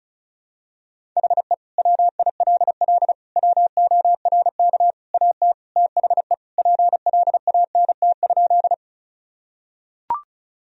Morse code, heard as a keyed tone, about 36 wpm, 700 Hz. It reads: HE WILL WORK AT THE PLANT?
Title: Sentences from Top 1000 Words - Review Part 1 - 35wpm